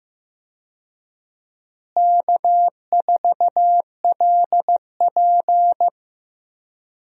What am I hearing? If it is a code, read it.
Morse code, heard as a keyed tone, about 15 wpm, 700 Hz. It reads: K4LP
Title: USA Call Signs - 4 Characters - Repeated 2x - 15wpm